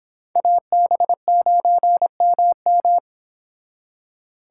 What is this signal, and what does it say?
Morse code, heard as a keyed tone, about 29 wpm, 700 Hz. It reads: AB9MM